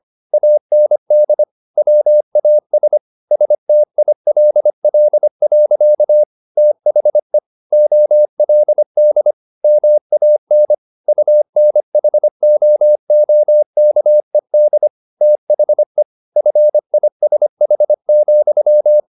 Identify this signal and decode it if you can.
Morse code, heard as a keyed tone, about 25 wpm, 600 Hz. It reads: AND WAS STILL. THE OLD MAN UNHOOKED THE FISH,